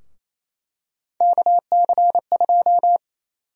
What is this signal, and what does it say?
Morse code, heard as a keyed tone, about 28 wpm, 700 Hz. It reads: KC2
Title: USA Call Sign Prefixes - 28wpm